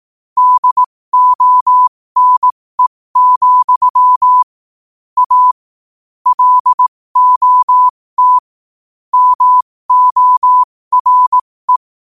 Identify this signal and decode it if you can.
Morse code, heard as a keyed tone, about 18 wpm, spaced as for 15 wpm, 1000 Hz. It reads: DONE, A LOT MORE
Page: New quotes every day in morse code at 15 Words per minute.